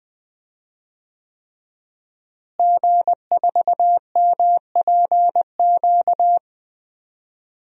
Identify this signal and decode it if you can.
Morse code, heard as a keyed tone, about 20 wpm, 700 Hz. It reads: G4MPQ
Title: DXCC-100 Call Signs - 5 Characters - Repeated 3x - 20wpm